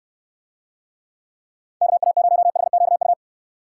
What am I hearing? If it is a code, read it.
Morse code, heard as a keyed tone, about 69 wpm, 700 Hz. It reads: KA0HQU